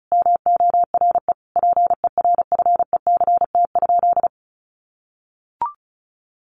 Morse code, heard as a keyed tone, about 34 wpm, 700 Hz. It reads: MORE PERFECT?
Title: Sentences from Top 1000 Words 35wpm